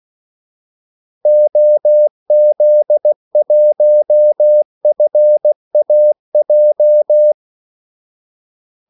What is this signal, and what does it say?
Morse code, heard as a keyed tone, about 16 wpm, 600 Hz. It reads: OZ1FAJ
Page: Kaldesignaler 16 wpm